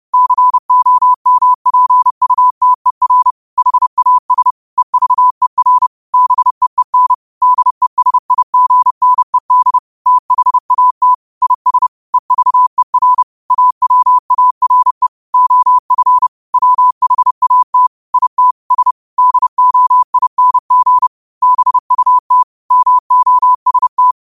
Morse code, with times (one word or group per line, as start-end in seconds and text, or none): 0.1-3.3 COMPUTER
3.6-4.5 HAS
4.8-5.9 EVER
6.1-7.1 BEEN
7.4-9.8 DESIGNED
10.1-11.1 THAT
11.4-11.9 IS
12.1-13.2 EVER
13.5-15.1 AWARE
15.3-16.3 OF
16.5-17.9 WHAT
18.1-18.5 IT
18.7-18.9 S
19.2-21.1 DOING
21.4-22.4 BUT
22.7-24.1 MOST